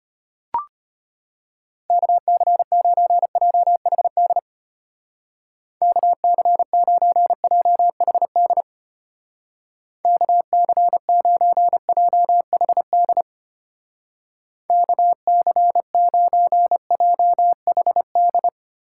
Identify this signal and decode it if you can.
Morse code, heard as a keyed tone, about 32 wpm, 700 Hz. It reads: KC9JHD KC9JHD KC9JHD KC9JHD